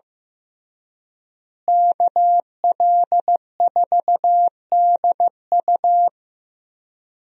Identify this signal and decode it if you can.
Morse code, heard as a keyed tone, about 16 wpm, 700 Hz. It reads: KL4DU
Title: USA Call Signs - 5 Characters - Repeated 3x - 15wpm